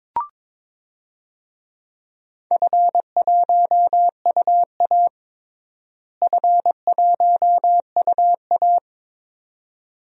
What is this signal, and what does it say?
Morse code, heard as a keyed tone, about 22 wpm, 700 Hz. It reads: F1UA F1UA
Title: DXCC-100 Call Signs - 4 Characters - Repeated 2x - 22wpm